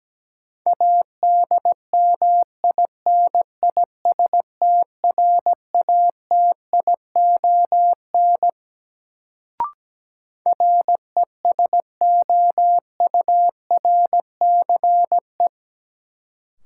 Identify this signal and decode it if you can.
Morse code, heard as a keyed tone, about 17 wpm, 700 Hz. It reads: ADMINISTRATION RESOURCE
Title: Sentences from Top 1300 Words - New Word Drill - 17wpm